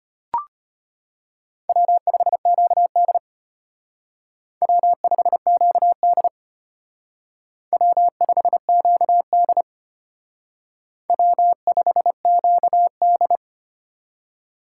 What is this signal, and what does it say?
Morse code, heard as a keyed tone, about 30 wpm, 700 Hz. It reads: W5QD W5QD W5QD W5QD
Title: Speed Racing - USA Call Signs 25wpm